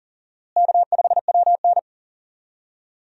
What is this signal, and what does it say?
Morse code, heard as a keyed tone, about 40 wpm, 700 Hz. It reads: K5WN
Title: USA Call Signs - 4 Characters - Repeated 3x - 40wpm